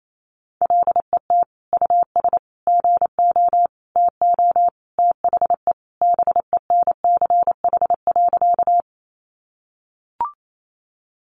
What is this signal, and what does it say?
Morse code, heard as a keyed tone, about 28 wpm, 700 Hz. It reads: LET US GO TO THE BENCH.
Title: Sentences from Top 1400 Words 28wpm